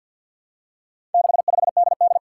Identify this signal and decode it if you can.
Morse code, heard as a keyed tone, about 48 wpm, 700 Hz. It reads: 65DD
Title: Four Challenging Letter-Number Combinations 50wpm